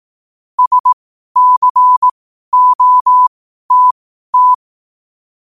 Morse code, heard as a keyed tone, about 18 wpm, spaced as for 12 wpm, 1000 Hz. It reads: SCOTT
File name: qotdfiles / 12wpm